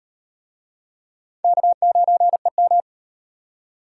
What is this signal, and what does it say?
Morse code, heard as a keyed tone, about 38 wpm, 700 Hz. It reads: K9EM